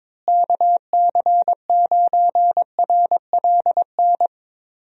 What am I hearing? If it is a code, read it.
Morse code, heard as a keyed tone, about 22 wpm, 700 Hz. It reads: KC9RLN
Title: USA Call Signs - 6 Characters - Repeated 2x - 22wpm